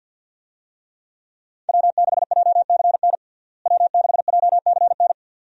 Morse code, heard as a keyed tone, about 50 wpm, 700 Hz. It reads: W6JCN W6JCN
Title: USA Call Signs - 5 Characters - Repeated 2x - 50wpm